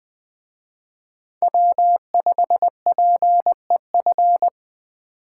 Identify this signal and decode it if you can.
Morse code, heard as a keyed tone, about 21 wpm, 700 Hz. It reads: W5PEF
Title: Speed Racing - USA Call Signs 17wpm